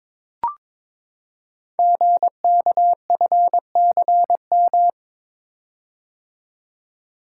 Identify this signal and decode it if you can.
Morse code, heard as a keyed tone, about 22 wpm, 700 Hz. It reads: GKFCM
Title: Code Groups - 5 Letters - 22wpm